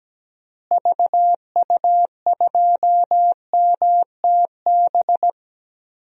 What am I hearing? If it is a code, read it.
Morse code, heard as a keyed tone, about 17 wpm, 700 Hz. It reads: VU2MTB
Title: DXCC-100 Call Signs - 6 Characters - Repeated 3x - 17wpm